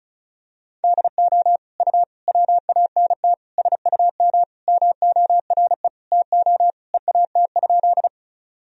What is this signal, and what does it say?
Morse code, heard as a keyed tone, about 36 wpm, 700 Hz. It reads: DO U WANT SUM MORE TO EAT?
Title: Sentences from Top 300 Words and Abbreviations 35wpm